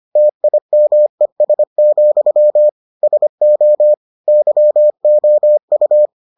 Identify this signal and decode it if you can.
Morse code, heard as a keyed tone, about 25 wpm, 600 Hz. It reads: TIMES, SO YOU